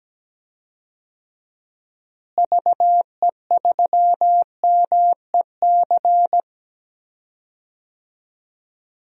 Morse code, heard as a keyed tone, about 17 wpm, 700 Hz. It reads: VE3MEC
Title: DXCC-100 Call Signs - 6 Characters - Repeated 3x - 17wpm